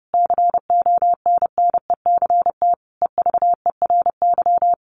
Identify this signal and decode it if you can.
Morse code, heard as a keyed tone, about 30 wpm, 700 Hz. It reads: CONNECT EVERY
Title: Sentences from Top 1500 Words 30wpm